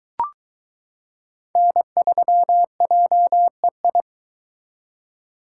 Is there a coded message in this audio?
Yes: Morse code from 1.6 to 4.0 s, heard as a keyed tone, about 22 wpm, 700 Hz. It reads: N3JEI